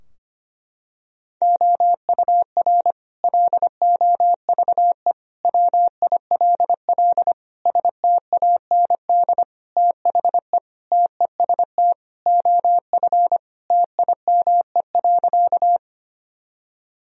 Morse code, heard as a keyed tone, about 25 wpm, 700 Hz. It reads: OUR LOVE WILL STAND THE TEST OF TIME.
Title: Sentences from Top 1000 Words - Review Part 3 - 25wpm